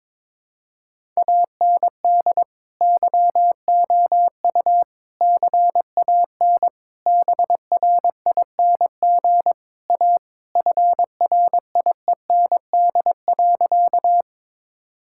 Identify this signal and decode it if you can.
Morse code, heard as a keyed tone, about 21 wpm, 700 Hz. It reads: AND YOU CAN BRING A FRIEND.
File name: practice / 22wpm / Sentences with 2-Word Phrases - 22wpm